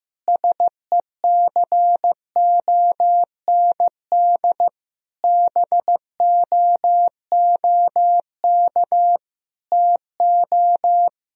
Morse code, heard as a keyed tone, about 16 wpm, 700 Hz. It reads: SECOND BOOK TO
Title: Sentences from Top 1000 Words - Review Part 1 - 15wpm